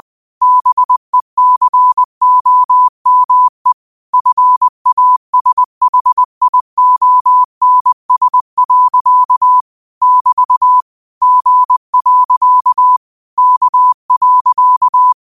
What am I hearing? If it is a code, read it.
Morse code, heard as a keyed tone, about 20 wpm, 1000 Hz. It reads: BECOME FASHIONS. <BT> G. K.